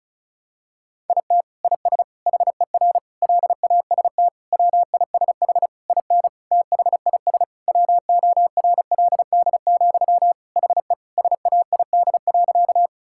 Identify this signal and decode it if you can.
Morse code, heard as a keyed tone, about 34 wpm, 700 Hz. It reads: IT IS HER LAST WISH IN THIS WORLD, HE SAID.
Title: Sentences from Top 1600 Words 35wpm